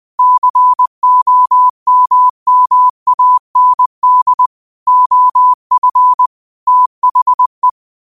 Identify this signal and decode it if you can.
Morse code, heard as a keyed tone, about 20 wpm, 1000 Hz. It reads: COMMAND OF THE
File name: qotdfiles / 20wpm